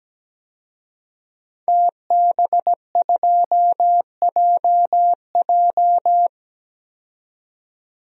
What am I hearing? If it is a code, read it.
Morse code, heard as a keyed tone, about 17 wpm, 700 Hz. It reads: TB2JJ
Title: DXCC-100 Call Signs - 5 Characters - 17wpm